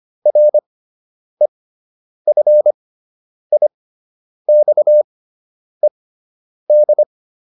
Morse code, heard as a keyed tone, about 25 wpm, spaced as for 9 wpm, 600 Hz. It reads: REFIXED